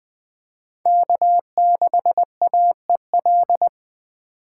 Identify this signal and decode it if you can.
Morse code, heard as a keyed tone, about 20 wpm, 700 Hz. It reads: K6AEL
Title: USA Call Signs - 5 Characters - 4x Character Spacing - 20wpm